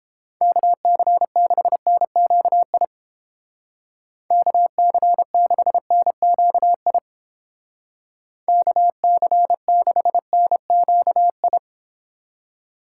Morse code, heard as a keyed tone, about 29 wpm, 700 Hz. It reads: KC6NQI KC6NQI KC6NQI